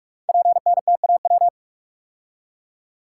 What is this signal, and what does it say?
Morse code, heard as a keyed tone, about 44 wpm, 690 Hz. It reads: PNTAW
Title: Code Groups - 5 Letters - 45wpm